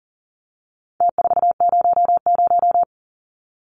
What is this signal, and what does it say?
Morse code, heard as a keyed tone, about 38 wpm, 700 Hz. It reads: T400
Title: Numbers - Contest Style - 4-Digit 40wpm